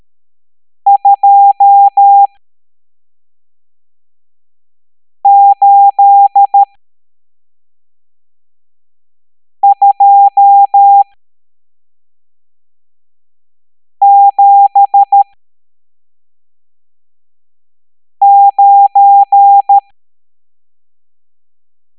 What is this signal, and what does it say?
Morse code, heard as a keyed tone, about 13 wpm, spaced as for 3 wpm, 800 Hz. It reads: 28279
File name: hamdata / L12-EX02-CHIFFRES